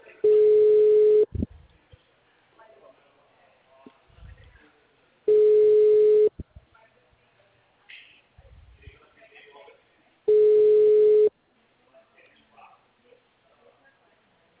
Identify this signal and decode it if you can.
Ringback tone (Europe, 425 Hz), with speech.